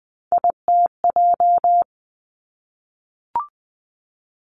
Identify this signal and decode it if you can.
Morse code, heard as a keyed tone, about 20 wpm, 700 Hz. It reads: ITJ
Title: Three Challenging Letter-Number Combinations 20wpm